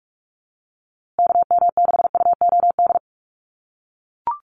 Morse code, heard as a keyed tone, about 44 wpm, 700 Hz. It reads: KM6UOD
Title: USA Call Signs Repeated 2x 45wpm